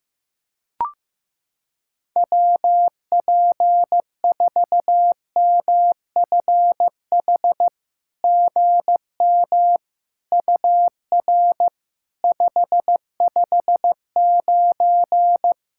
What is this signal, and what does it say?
Morse code, heard as a keyed tone, about 15 wpm, 700 Hz. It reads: WP4MFH GM UR 559
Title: SOTA QSO Practice 15wpm